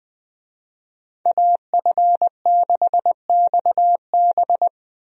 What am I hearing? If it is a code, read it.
Morse code, heard as a keyed tone, about 20 wpm, 700 Hz. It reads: AF6XB
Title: USA Call Signs - 5 Characters - 20wpm